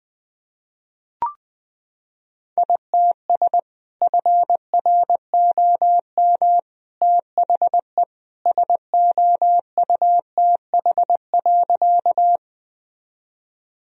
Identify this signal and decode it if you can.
Morse code, heard as a keyed tone, about 20 wpm, 700 Hz. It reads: ITS FROM THE SOUTH.
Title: Sentences from Top 1000 Words - Review Part 2 - 20wpm